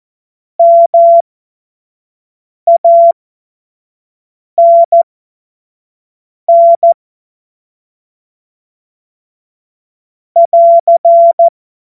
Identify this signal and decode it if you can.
Morse code, heard as a keyed tone, about 14 wpm, spaced as for 5 wpm, 680 Hz. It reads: MANN <AR>
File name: Morsekurs / Witze / Witz 12. April 2022 25 F70